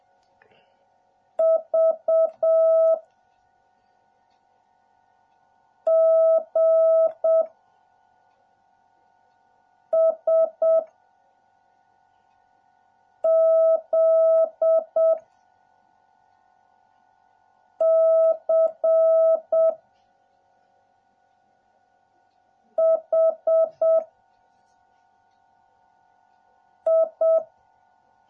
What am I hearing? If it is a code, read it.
Morse code, heard as a keyed tone, about 7 wpm, 660 Hz. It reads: V G S Z C H I